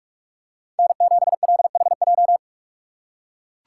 Morse code, heard as a keyed tone, about 42 wpm, 700 Hz. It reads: N7LHJ
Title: USA Call Signs - 5 Characters - 6x Character Spacing - 45wpm